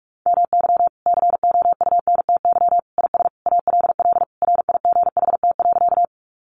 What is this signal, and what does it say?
Morse code, heard as a keyed tone, about 44 wpm, 700 Hz. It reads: MY COUNTY IS ALL RIGHT.
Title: Sentences from Top 1200 Words 45wpm